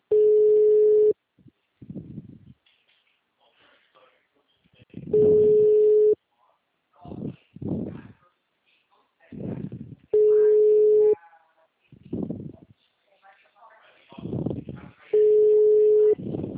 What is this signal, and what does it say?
Ringback tone (Europe, 425 Hz), with speech.